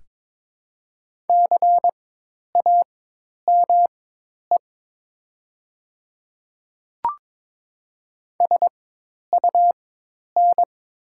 Morse code, heard as a keyed tone, about 23 wpm, spaced as for 10 wpm, 700 Hz. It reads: CAME SUN
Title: 4x Character Spacing - Top 300 Words - 22wpm